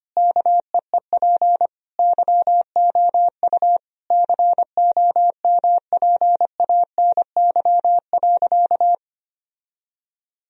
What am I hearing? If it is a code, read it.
Morse code, heard as a keyed tone, about 25 wpm, 700 Hz. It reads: KEEP YOU COMPANY.
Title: Sentences from Top 1000 Words - Review Part 3 - 25wpm